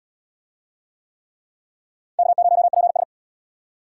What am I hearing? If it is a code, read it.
Morse code, heard as a keyed tone, about 68 wpm, 700 Hz. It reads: K0PS